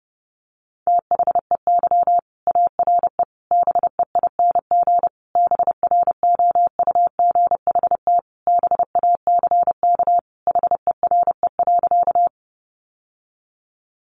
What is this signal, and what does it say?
Morse code, heard as a keyed tone, about 30 wpm, 700 Hz. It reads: THEY ARE BEING BROUGHT BACK HERE.
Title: Sentences from Top 600 Words 30wpm